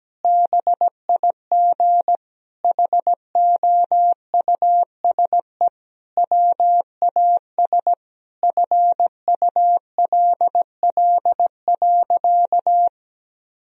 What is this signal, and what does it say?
Morse code, heard as a keyed tone, about 17 wpm, 700 Hz. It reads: BIG HOUSE WAS FULL.
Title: Sentences from Top 1000 Words - Review Part 3 - 17wpm